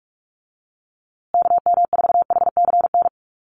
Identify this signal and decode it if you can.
Morse code, heard as a keyed tone, about 44 wpm, 700 Hz. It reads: KM4HCN
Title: USA Call Signs Repeated 2x 45wpm